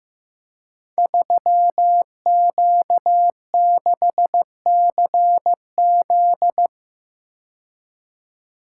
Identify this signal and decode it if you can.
Morse code, heard as a keyed tone, about 15 wpm, 700 Hz. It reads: 3Q6CZ
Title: Code Groups - 5 Letter:Numbers - 15wpm